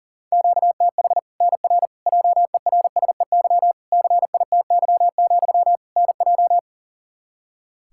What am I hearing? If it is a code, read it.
Morse code, heard as a keyed tone, about 40 wpm, 700 Hz. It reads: QTH NR JERSEY CITY, NJ
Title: K1USN SST QSO Practice 40wpm